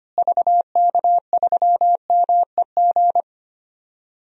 Morse code, heard as a keyed tone, about 25 wpm, 700 Hz. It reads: VK3MEG
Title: DXCC-100 Call Signs - 6 Characters - Repeated 3x - 25wpm